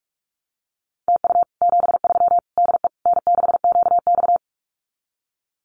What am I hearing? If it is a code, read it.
Morse code, heard as a keyed tone, about 46 wpm, 700 Hz. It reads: TU 73 DE N6QX
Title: Common QSO Phrases Encore 45wpm